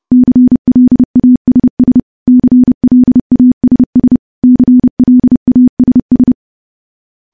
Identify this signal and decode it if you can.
Morse code, heard as a keyed tone, about 30 wpm, 250 Hz. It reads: CLASS CLASS CLASS
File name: morse-training-30WPM